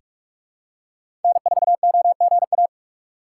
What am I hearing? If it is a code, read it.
Morse code, heard as a keyed tone, about 46 wpm, 700 Hz. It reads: N4OGA
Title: USA Call Signs - 5 Characters - 4x Character Spacing - 45wpm